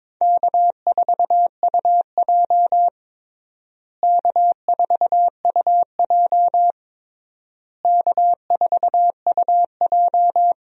Morse code, heard as a keyed tone, about 22 wpm, 700 Hz. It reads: K4UJ K4UJ K4UJ
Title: USA Call Signs - 4 Characters - Repeated 3x - 22wpm